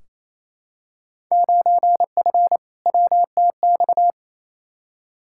Morse code, heard as a keyed tone, about 28 wpm, 700 Hz. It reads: 9F WTX
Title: Field Day Exchanges Encore - 28wpm